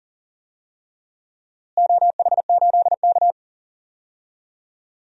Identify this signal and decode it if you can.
Morse code, heard as a keyed tone, about 39 wpm, 700 Hz. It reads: OH8K
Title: DXCC-100 Call Signs - 4 Characters - Repeated 3x - 40wpm